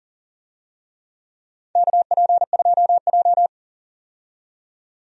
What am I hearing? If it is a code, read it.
Morse code, heard as a keyed tone, about 40 wpm, 700 Hz. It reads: KP2J
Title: DXCC-100 Call Signs - 4 Characters - 40wpm